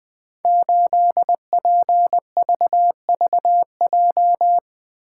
Morse code, heard as a keyed tone, about 20 wpm, 700 Hz. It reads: 8PVVJ
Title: Code Groups - 5 Letter:Numbers - 20wpm